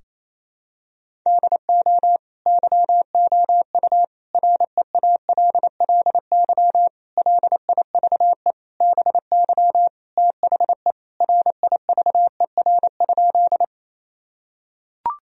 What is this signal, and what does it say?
Morse code, heard as a keyed tone, about 28 wpm, 700 Hz. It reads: DO YOU REALLY LIVE BY THE RIVER?
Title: Sentences from Top 1000 Words - Review Part 1 - 28wpm